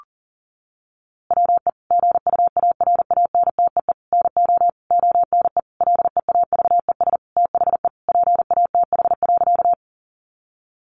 Morse code, heard as a keyed tone, about 40 wpm, 700 Hz. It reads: WE GUARANTEE NO ONE LEAVES THE PATH.
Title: A Sound Of Thunder 40wpm